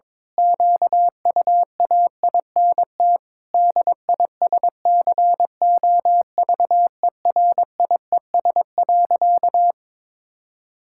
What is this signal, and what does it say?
Morse code, heard as a keyed tone, about 22 wpm, 700 Hz. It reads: QUAINT DISCOVERIES.
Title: Pangram Challenge v2 - 22wpm